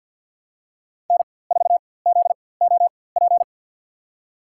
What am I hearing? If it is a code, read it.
Morse code, heard as a keyed tone, about 53 wpm, spaced as for 24 wpm, 700 Hz. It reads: N4ZOP
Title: USA Call Signs - 5 Characters - 4x Character Spacing - 50wpm